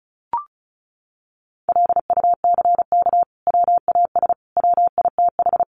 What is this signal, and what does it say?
Morse code, heard as a keyed tone, about 34 wpm, 700 Hz. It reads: LUCK WAS WITH
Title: Sentences from Top 1500 Words 35wpm